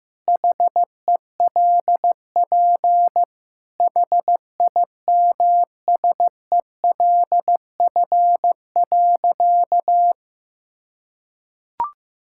Morse code, heard as a keyed tone, about 16 wpm, 700 Hz. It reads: HELP HIMSELF.
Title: Sentences from Top 1600 Words 15wpm